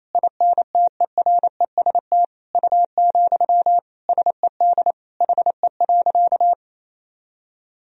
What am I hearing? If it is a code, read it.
Morse code, heard as a keyed tone, about 27 wpm, 700 Hz. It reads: INTEREST U, SED HE.
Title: Sentences from Top 400 Words and Abbreviations 28wpm